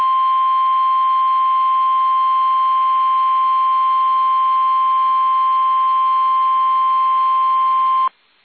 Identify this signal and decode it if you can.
Emergency Alert System (SAME) tones (the 1050 Hz attention tone). It is heard from 0.6 to 8.0 s.